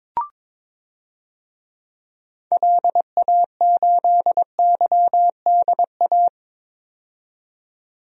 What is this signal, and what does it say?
Morse code, heard as a keyed tone, about 21 wpm, 700 Hz. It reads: LA8YDA
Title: DXCC-100 Call Signs - 6 Characters - 22wpm